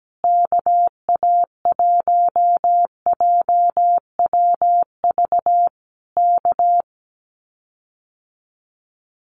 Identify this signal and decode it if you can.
Morse code, heard as a keyed tone, about 17 wpm, 700 Hz. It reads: KA1JWV K
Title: Common QSO Phrases Encore 17wpm